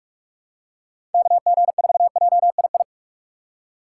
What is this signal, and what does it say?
Morse code, heard as a keyed tone, about 46 wpm, 700 Hz. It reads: KG4JII
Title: USA Call Signs - 6 Characters - 2x Character Spacing - 45wpm